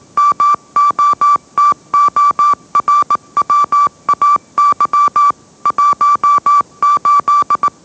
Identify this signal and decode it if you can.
Morse code, heard as a keyed tone, about 18 wpm, 1160 Hz. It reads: MOTORWAY 18
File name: mrx / mrx082 20wpm